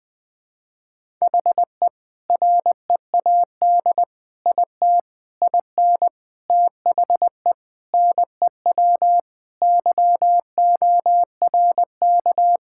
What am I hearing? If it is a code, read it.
Morse code, heard as a keyed tone, about 21 wpm, 700 Hz. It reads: HE READ IT IN THE NEW YORK
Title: Sentences with 4-Word Phrases - Spoken Beginning - 20wpm